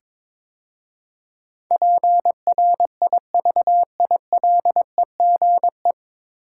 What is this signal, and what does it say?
Morse code, heard as a keyed tone, about 23 wpm, 690 Hz. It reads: PRIVILEGE
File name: 9 Letter Words - First 3 Letters Given - 22wpm